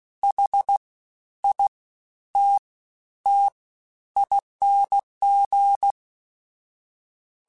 Morse code, heard as a keyed tone, about 16 wpm, spaced as for 10 wpm, 800 Hz. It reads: HITTING
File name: Special INGs-mp3s / Special INGs-mp3s / SlowToFaster ing @15